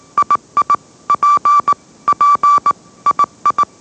Morse code, heard as a keyed tone, about 18 wpm, 1160 Hz. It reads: II P P II
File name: mrx / mrx082 20wpm